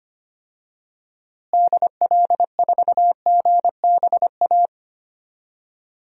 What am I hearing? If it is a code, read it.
Morse code, heard as a keyed tone, about 25 wpm, 700 Hz. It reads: DL4GBA
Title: DXCC-100 Call Signs - 6 Characters - Repeated 3x - 25wpm